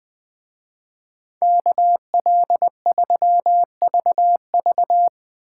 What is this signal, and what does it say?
Morse code, heard as a keyed tone, about 20 wpm, 700 Hz. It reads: KL3VV
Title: Speed Racing - USA 2x2 Call Signs 17wpm